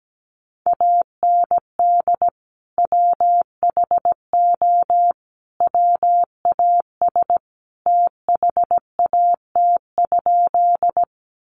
Morse code, heard as a keyed tone, about 17 wpm, 700 Hz. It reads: AND WHO WAS THAT?
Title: Sentences from Top 100 Words 17wpm